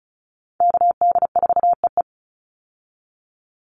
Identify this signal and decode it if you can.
Morse code, heard as a keyed tone, about 34 wpm, 700 Hz. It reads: KD4EE
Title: Call Signs Repeated 3x - 35wpm